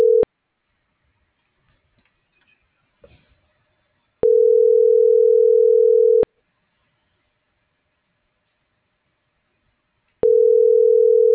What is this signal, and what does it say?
Ringback tone (North America), 440 + 480 Hz.